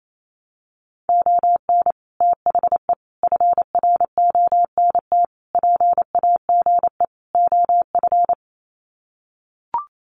Morse code, heard as a keyed tone, about 28 wpm, 700 Hz. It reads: ON THE FRONT PAGE OF
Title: Sets of 5 Words Encore 28wpm